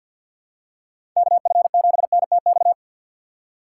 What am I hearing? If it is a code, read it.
Morse code, heard as a keyed tone, about 50 wpm, 700 Hz. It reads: KF7NTX
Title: USA Call Signs - 6 Characters - 3x Character Spacing - 50wpm